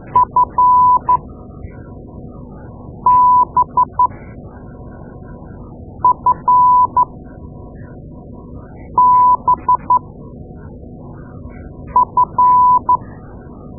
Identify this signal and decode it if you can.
Morse code, heard as a keyed tone, about 12 wpm, spaced as for 4 wpm, 1000 Hz. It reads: FBFBF
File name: Morse Lessons / Morse Lessons / Tape-2A